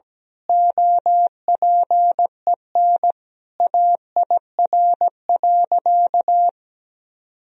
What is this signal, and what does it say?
Morse code, heard as a keyed tone, about 17 wpm, 700 Hz. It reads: OPEN AIR.
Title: Sentences from Top 1000 Words - Review Part 1 - 17wpm